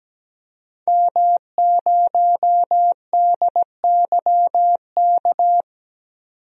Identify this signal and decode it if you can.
Morse code, heard as a keyed tone, about 17 wpm, 700 Hz. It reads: M0DYK
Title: DXCC-100 Call Signs - 5 Characters - 17wpm